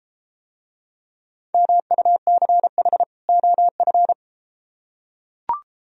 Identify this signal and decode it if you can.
Morse code, heard as a keyed tone, about 32 wpm, 700 Hz. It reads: MUCH OF